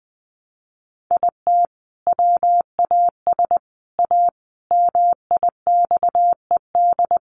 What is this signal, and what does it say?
Morse code, heard as a keyed tone, about 20 wpm, 700 Hz. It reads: IT WAS A MIXED
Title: Sentences from Top 1500 Words 20wpm